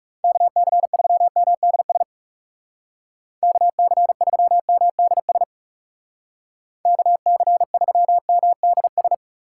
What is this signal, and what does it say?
Morse code, heard as a keyed tone, about 38 wpm, 700 Hz. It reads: KC3MDS KC3MDS KC3MDS